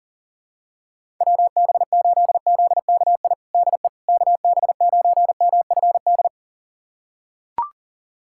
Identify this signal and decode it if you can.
Morse code, heard as a keyed tone, about 40 wpm, 700 Hz. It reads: WB8ZKI DE KB9MFD
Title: USA Callsign DE Callsign - 6 Characters - 40wpm